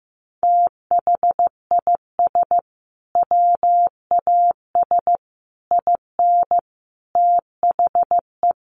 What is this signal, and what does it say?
Morse code, heard as a keyed tone, about 15 wpm, 700 Hz. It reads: THIS WAS IN THE
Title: Sentences from Top 1300 Words 15wpm